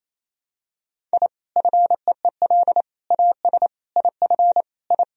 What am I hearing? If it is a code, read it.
Morse code, heard as a keyed tone, about 28 wpm, 700 Hz. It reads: I FEEL AS IF I
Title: Sentences from Top 1000 Words - Review Part 3 - 28wpm